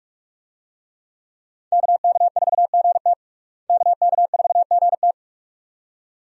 Morse code, heard as a keyed tone, about 46 wpm, 700 Hz. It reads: KK4GT KK4GT
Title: USA Call Signs - 5 Characters - Repeated 2x - 45wpm